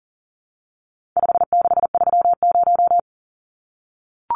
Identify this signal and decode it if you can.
Morse code, heard as a keyed tone, about 40 wpm, 700 Hz. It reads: 5630
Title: Numbers - 4-Digit 40wpm